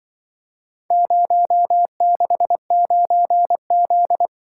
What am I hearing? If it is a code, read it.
Morse code, heard as a keyed tone, about 24 wpm, 700 Hz. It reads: 069Z
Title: Speed-Racing - Four Letter-Number Combinations - 20wpm